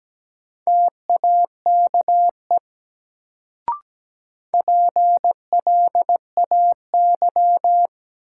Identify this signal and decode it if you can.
Morse code, heard as a keyed tone, about 17 wpm, 700 Hz. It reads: TAKE PLAY
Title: Sentences from Top 200 Words - New Word Drill - 17wpm